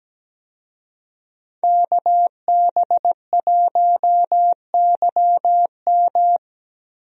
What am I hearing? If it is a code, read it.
Morse code, heard as a keyed tone, about 17 wpm, 700 Hz. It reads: KB1YM
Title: USA Call Signs - 5 Characters - Repeated 2x - 17wpm